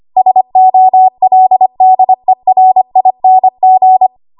Morse code, heard as a keyed tone, about 25 wpm, 750 Hz. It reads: SOLDERING
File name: Archive / 25 WPM / 25WPM